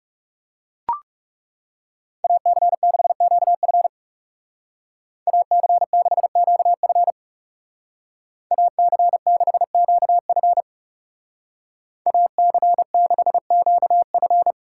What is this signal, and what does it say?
Morse code, heard as a keyed tone, about 36 wpm, 700 Hz. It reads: AC6QF AC6QF AC6QF AC6QF
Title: Speed Racing - USA 2x2 Call Signs 30wpm